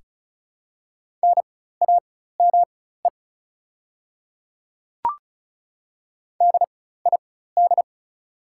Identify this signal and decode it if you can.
Morse code, heard as a keyed tone, about 34 wpm, spaced as for 16 wpm, 700 Hz. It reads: NAME DID
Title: 4x Character Spacing - Top 200 Words - 35wpm